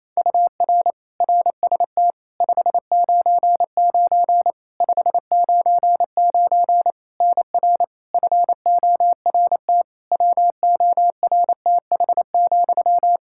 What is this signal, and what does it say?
Morse code, heard as a keyed tone, about 28 wpm, 700 Hz. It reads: UR RST 599 599 NR FORT WORTH,
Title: Practice QSO - Exchange 1 - Beginning - 28wpm